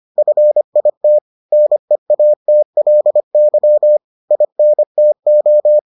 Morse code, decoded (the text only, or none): FIT NEATLY INTO